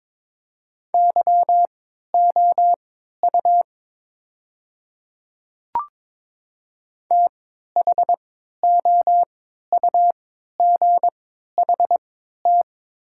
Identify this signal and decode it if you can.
Morse code, heard as a keyed tone, about 22 wpm, spaced as for 13 wpm, 700 Hz. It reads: YOU THOUGHT
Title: 3x Character Spacing - Top 300 Words - 22wpm